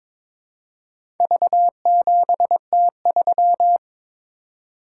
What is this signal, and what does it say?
Morse code, heard as a keyed tone, about 22 wpm, 700 Hz. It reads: V7T3
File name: practice / 22wpm / Four Letter-Number Combinations - 22wpm